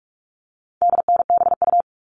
Morse code, heard as a keyed tone, about 44 wpm, 700 Hz. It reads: DNBU
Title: All Cut Numbers - 4-Digit 45wpm